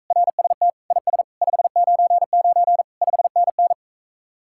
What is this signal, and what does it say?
Morse code, heard as a keyed tone, about 46 wpm, 700 Hz. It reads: RST IS 599 5NN